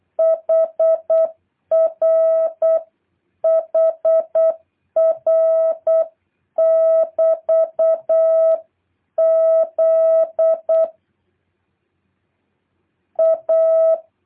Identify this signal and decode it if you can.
Morse code, heard as a keyed tone, about 8 wpm, 650 Hz. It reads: HR HR <BT>Z A
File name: jh8cbh / 2023cw